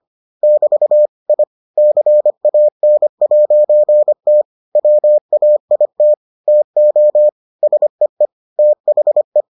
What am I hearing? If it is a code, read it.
Morse code, heard as a keyed tone, about 25 wpm, 600 Hz. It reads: <BT> I CAN'T WAIT TO SEE THE